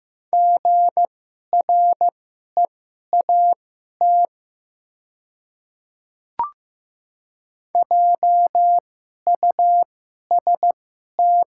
Morse code, heard as a keyed tone, about 15 wpm, spaced as for 11 wpm, 700 Hz. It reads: GREAT JUST
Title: 2x Character Spacing - Top 300 Words - 15wpm